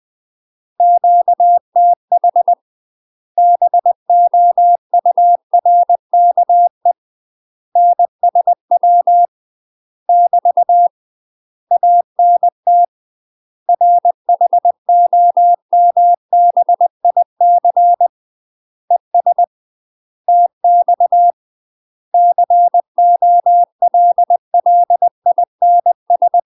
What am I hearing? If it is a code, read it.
Morse code, heard as a keyed tone, about 20 wpm, 700 Hz. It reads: QTH BOURKE NSW <BT> ANT RHOMBIC ES TX COLLINS
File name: UK MASTER QSOs TRAINING / 20WPM / UK MASTER QSOs 0001